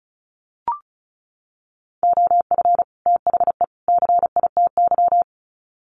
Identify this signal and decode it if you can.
Morse code, heard as a keyed tone, about 34 wpm, 700 Hz. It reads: OF THE CITY
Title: Sets of 3 Words 35wpm